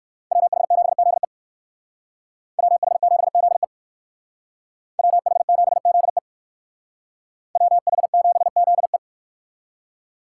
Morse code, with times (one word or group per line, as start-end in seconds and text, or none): 0.3-1.2 WH7ZE
2.6-3.6 WH7ZE
5.0-6.2 WH7ZE
7.6-9.0 WH7ZE